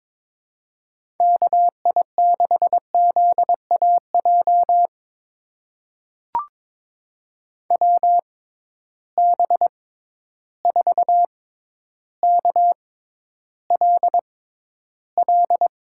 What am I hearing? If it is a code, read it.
Morse code, heard as a keyed tone, about 22 wpm, spaced as for 11 wpm, 700 Hz. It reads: KI6ZAJ WB4KLL
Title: USA Call Signs - 6 Characters - 6x Character Spacing - 22wpm